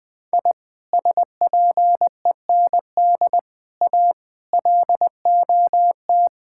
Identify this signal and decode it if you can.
Morse code, heard as a keyed tone, about 20 wpm, 700 Hz. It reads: I SPEND A LOT
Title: Sentences with 5-Word Phrases - 20wpm